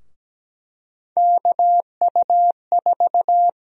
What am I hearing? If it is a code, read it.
Morse code, heard as a keyed tone, about 17 wpm, 700 Hz. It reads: KU4
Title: USA Call Sign Prefixes - 17wpm